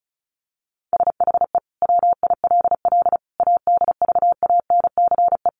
Morse code, heard as a keyed tone, about 34 wpm, 700 Hz. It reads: SHE WILL ADVANCE
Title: Sentences from Top 1400 Words 35wpm